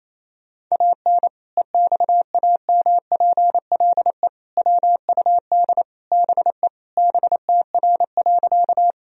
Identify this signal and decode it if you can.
Morse code, heard as a keyed tone, about 28 wpm, 700 Hz. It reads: AN EXAMPLE WUD BE BTR.
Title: Sentences from Top 400 Words and Abbreviations 28wpm